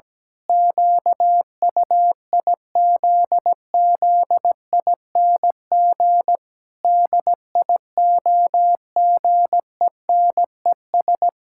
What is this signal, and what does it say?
Morse code, heard as a keyed tone, about 17 wpm, 700 Hz. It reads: QUIZZING DIOGENES
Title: Pangram Challenge v2 - 17wpm